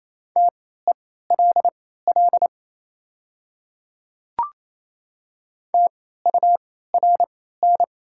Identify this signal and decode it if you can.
Morse code, heard as a keyed tone, about 28 wpm, spaced as for 16 wpm, 700 Hz. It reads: TELL TURN
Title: 3x Character Spacing - Top 200 Words - 28wpm